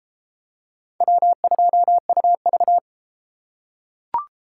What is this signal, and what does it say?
Morse code, heard as a keyed tone, about 30 wpm, 700 Hz. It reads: W2UV